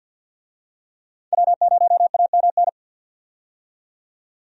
Morse code, heard as a keyed tone, about 52 wpm, 700 Hz. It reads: W0AMN
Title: USA Call Signs - 5 Characters - 50wpm